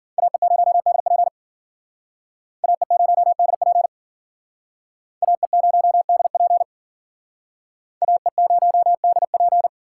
Morse code, heard as a keyed tone, about 51 wpm, 700 Hz. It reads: AE0DP AE0DP AE0DP AE0DP